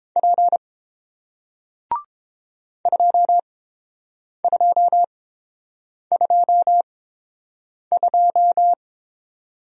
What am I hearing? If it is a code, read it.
Morse code, heard as a keyed tone, about 29 wpm, spaced as for 7 wpm, 700 Hz. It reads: P 2222